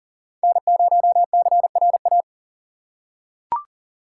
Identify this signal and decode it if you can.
Morse code, heard as a keyed tone, about 40 wpm, 700 Hz. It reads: N0CRA
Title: USA Call Signs - 3x Character Spacing - 40wpm